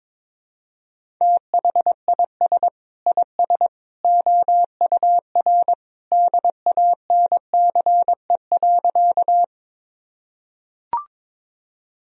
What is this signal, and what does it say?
Morse code, heard as a keyed tone, about 22 wpm, 700 Hz. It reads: THIS IS OUR DANCE.
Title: Sentences from Top 1000 Words - Review Part 4 - 22wpm